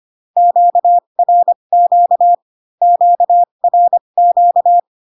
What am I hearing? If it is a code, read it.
Morse code, heard as a keyed tone, about 25 wpm, spaced as for 22 wpm, 700 Hz. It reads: QRQ QRQ
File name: Q-codes most common repeated 3 times 18 25wpm